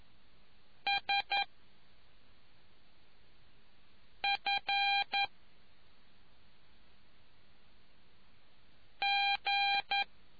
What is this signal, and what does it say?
Morse code, heard as a keyed tone, about 11 wpm, 800 Hz. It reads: S F G